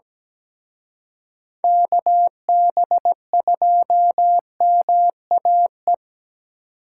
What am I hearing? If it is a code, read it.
Morse code, heard as a keyed tone, about 20 wpm, 700 Hz. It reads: KB2MAE
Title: Speed Racing - USA 2x3 Call Signs 17wpm